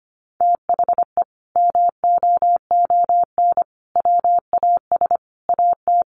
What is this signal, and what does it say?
Morse code, heard as a keyed tone, about 25 wpm, 700 Hz. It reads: THE MOON WAS AT
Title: Sentences from Top 1300 Words 25wpm